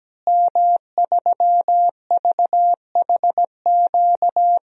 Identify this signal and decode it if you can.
Morse code, heard as a keyed tone, about 17 wpm, 700 Hz. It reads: M3VHQ
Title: DXCC-100 Call Signs - 5 Characters - Repeated 2x - 17wpm